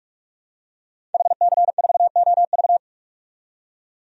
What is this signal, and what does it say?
Morse code, heard as a keyed tone, about 41 wpm, 700 Hz. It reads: HC4OV
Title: DXCC-100 Call Signs - 5 Characters - 45wpm